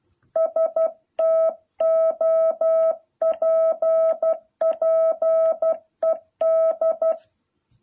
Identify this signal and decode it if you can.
Morse code, heard as a keyed tone, about 12 wpm, 650 Hz. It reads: STOPPED
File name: jh8cbh / 2021cw